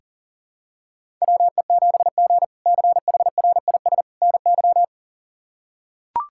Morse code, heard as a keyed tone, about 40 wpm, 700 Hz. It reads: WE7G CHRIS NY
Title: K1USN SST QSO Practice - 40wpm